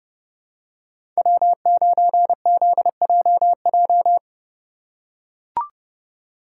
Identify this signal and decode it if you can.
Morse code, heard as a keyed tone, about 32 wpm, 700 Hz. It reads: W9ZJJ
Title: USA Call Signs - 5 Characters - 2x Character Spacing - 30wpm